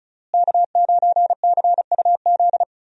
Morse code, heard as a keyed tone, about 34 wpm, 700 Hz. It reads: K9CUZ
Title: USA Call Signs - 5 Characters - Repeated 2x - 35wpm